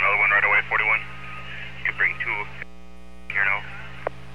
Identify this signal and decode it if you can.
Emergency Alert System (SAME) tones, heard as harsh buzzing bursts, with speech. It is heard from 0.4 to 3.6 s.